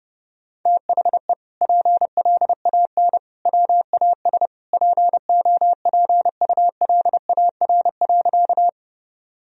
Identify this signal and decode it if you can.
Morse code, heard as a keyed tone, about 30 wpm, 700 Hz. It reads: THE PLAN WAS POPULAR.
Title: Sentences from Top 1000 Words - Review Part 3 - 30wpm